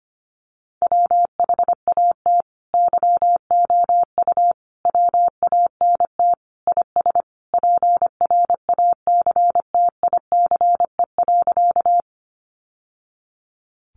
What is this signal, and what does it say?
Morse code, heard as a keyed tone, about 25 wpm, 700 Hz. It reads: WHAT YOU WANT IS PRACTICE.
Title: Sentences from Top 700 Words 25wpm